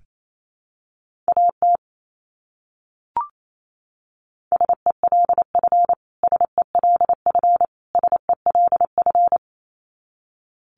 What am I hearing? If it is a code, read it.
Morse code, heard as a keyed tone, about 28 wpm, 700 Hz. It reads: AT SELF SELF SELF
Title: Top 200 Words Repeated 28wpm